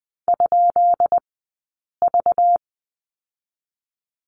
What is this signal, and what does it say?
Morse code, heard as a keyed tone, about 20 wpm, spaced as for 8 wpm, 700 Hz. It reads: ?V
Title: 197 Lesson 32d - Letter k - Two Characters 20wpm